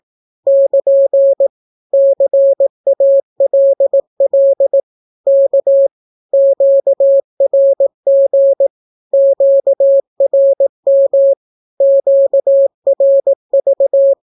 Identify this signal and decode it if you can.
Morse code, heard as a keyed tone, about 18 wpm, 550 Hz. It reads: <KN> CALL K QRG QRM QRV
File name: naukapodstaw / skroty / skroty